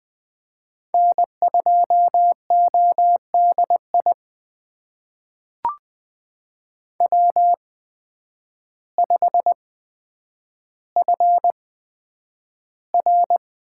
Morse code, heard as a keyed tone, about 20 wpm, spaced as for 9 wpm, 700 Hz. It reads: N2ODI W5FR
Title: USA Call Signs - 8x Character Spacing - 20wpm